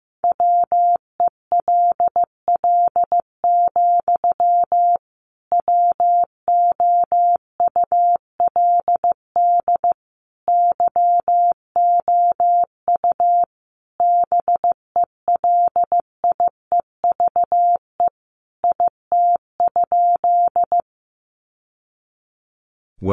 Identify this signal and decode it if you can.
Morse code, heard as a keyed tone, about 15 wpm, 700 Hz. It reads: WELL, WOULD YOU BELIEVE IT?
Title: Sentences from Top 1000 Words 15wpm